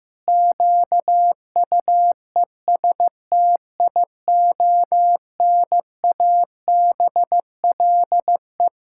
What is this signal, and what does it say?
Morse code, heard as a keyed tone, about 15 wpm, 700 Hz. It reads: QUESTIONABLE
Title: Pangram Challenge v2 - 15wpm